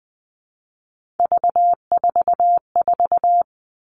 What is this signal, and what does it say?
Morse code, heard as a keyed tone, about 20 wpm, 700 Hz. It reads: V44
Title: Three Challenging Letter-Number Combinations 20wpm